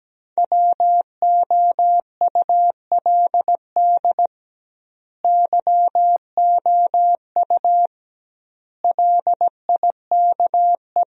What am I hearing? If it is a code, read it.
Morse code, heard as a keyed tone, about 17 wpm, 700 Hz. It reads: WOULD YOU LIKE
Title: 2x Word-Spacing - Sentences from Top 100 Words - 17wpm